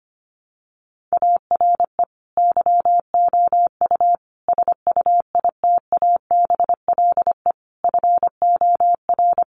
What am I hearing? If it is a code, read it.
Morse code, heard as a keyed tone, about 25 wpm, 700 Hz. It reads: ARE YOU SUITABLE FOR
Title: Sentences from Top 1400 Words 25wpm